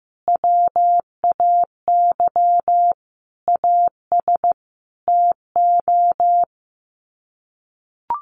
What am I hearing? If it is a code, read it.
Morse code, heard as a keyed tone, about 14 wpm, 700 Hz. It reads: WAY AS TO
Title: Sets of 5 Words Encore2 15wpm